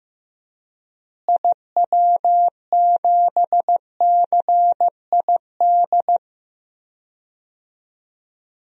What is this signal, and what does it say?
Morse code, heard as a keyed tone, about 16 wpm, 700 Hz. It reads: IW7CID
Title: DXCC-100 Call Signs - 6 Characters - Repeated 2x - 15wpm